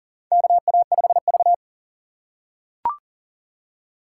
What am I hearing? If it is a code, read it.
Morse code, heard as a keyed tone, about 40 wpm, 700 Hz. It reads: KA5V
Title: USA Call Signs - 4 Characters - 3x Character Spacing - 40wpm